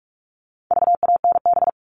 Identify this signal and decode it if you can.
Morse code, heard as a keyed tone, about 44 wpm, 700 Hz. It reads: VANB
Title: All Cut Numbers - 4-Digit 45wpm